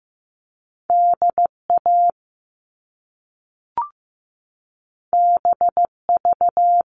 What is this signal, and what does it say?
Morse code, heard as a keyed tone, about 15 wpm, 700 Hz. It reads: DA BV
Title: All Cut Numbers - 2-Digit 15wpm